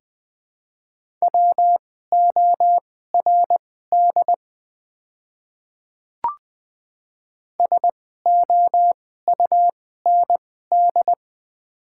Morse code, heard as a keyed tone, about 20 wpm, spaced as for 14 wpm, 700 Hz. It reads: WORD SOUND
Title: 2x Character Spacing - Top 300 Words - 20wpm